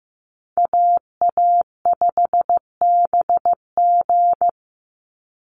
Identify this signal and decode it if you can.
Morse code, heard as a keyed tone, about 15 wpm, 700 Hz. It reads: AA5BG
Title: Call Signs Repeated 3x - 15wpm